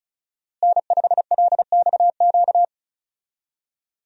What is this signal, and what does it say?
Morse code, heard as a keyed tone, about 36 wpm, 700 Hz. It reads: N5LXQ
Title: USA Call Signs - 5 Characters - Repeated 2x - 35wpm